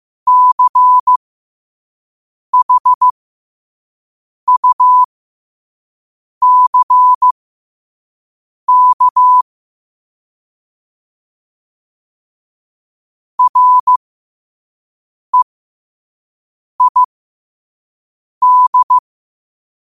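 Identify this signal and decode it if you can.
Morse code, heard as a keyed tone, about 15 wpm, spaced as for 5 wpm, 1000 Hz. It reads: CHUCK REID
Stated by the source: Morse quote of the day at 5 wpm.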